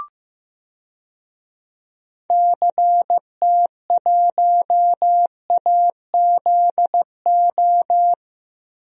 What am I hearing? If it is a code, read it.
Morse code, heard as a keyed tone, about 16 wpm, 700 Hz. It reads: CT1AZO
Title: DXCC-100 Call Signs - 6 Characters - 15wpm